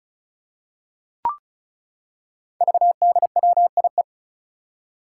Morse code, heard as a keyed tone, about 36 wpm, 700 Hz. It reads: VDWIE
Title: Five Challenging Letter-Number Combinations 35wpm